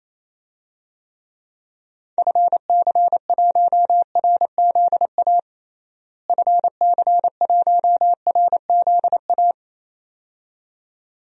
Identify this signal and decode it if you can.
Morse code, heard as a keyed tone, about 28 wpm, 700 Hz. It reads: FC1RZA FC1RZA
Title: DXCC-100 Call Signs - 6 Characters - Repeated 2x - 28wpm